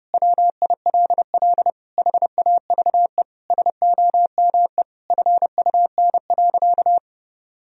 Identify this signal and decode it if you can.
Morse code, heard as a keyed tone, about 32 wpm, 700 Hz. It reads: WILL HAVE SOME FUN.
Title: Sentences from Top 1000 Words - Review Part 4 - 30wpm